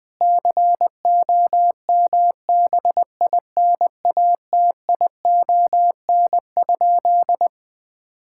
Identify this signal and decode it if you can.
Morse code, heard as a keyed tone, about 20 wpm, 700 Hz. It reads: COMBINATION?
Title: Sentences from Top 1000 Words - Review Part 6 - 20wpm